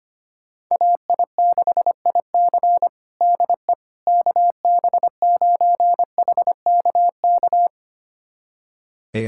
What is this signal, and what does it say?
Morse code, heard as a keyed tone, about 25 wpm, 700 Hz. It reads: AI6IC DE KB9HKK
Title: USA Callsign DE Callsign 25wpm